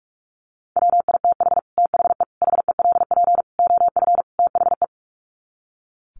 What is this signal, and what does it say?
Morse code, heard as a keyed tone, about 44 wpm, 700 Hz. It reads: WITH THE HELP OF THE
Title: Sets of 5 Words Encore 45wpm